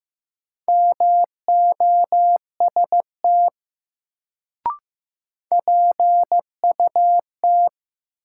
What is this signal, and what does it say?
Morse code, heard as a keyed tone, about 15 wpm, 700 Hz. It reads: MOST PUT
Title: Sentences from Top 200 Words - New Word Drill - 15wpm